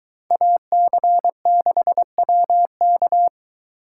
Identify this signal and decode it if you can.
Morse code, heard as a keyed tone, about 21 wpm, 700 Hz. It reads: AC6WK